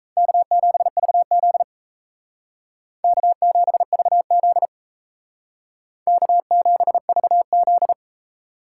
Morse code, heard as a keyed tone, about 36 wpm, 700 Hz. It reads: K7VZ K7VZ K7VZ